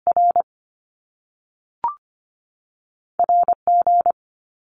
Morse code, heard as a keyed tone, about 25 wpm, 700 Hz. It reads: R R G
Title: Top Two Letter Combinations in Words 25wpm